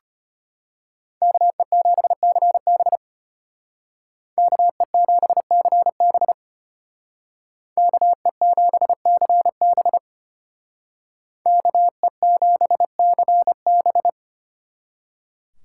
Morse code, heard as a keyed tone, about 31 wpm, 700 Hz. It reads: KE7CB KE7CB KE7CB KE7CB
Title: Speed Racing - USA Call Signs 25wpm